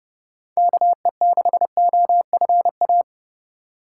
Morse code, heard as a keyed tone, about 31 wpm, 700 Hz. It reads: KE6OFA
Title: USA Call Signs - 3x Character Spacing - 30wpm